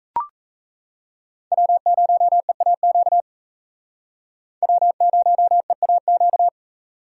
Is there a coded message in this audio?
Yes: Morse code from 1.5 to 6.5 s, heard as a keyed tone, about 35 wpm, 700 Hz. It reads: W0EAQ W0EAQ